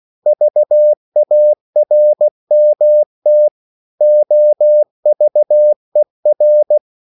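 Morse code, heard as a keyed tone, about 16 wpm, 600 Hz. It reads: VARMT OVER
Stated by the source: Slow (QRS) Morse code at 16 wpm, from CW med Gnister.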